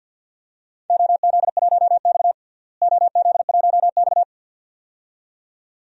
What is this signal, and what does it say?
Morse code, heard as a keyed tone, about 50 wpm, 700 Hz. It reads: OZ1X OZ1X
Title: DXCC-100 Call Signs - 4 Characters - Repeated 2x - 50wpm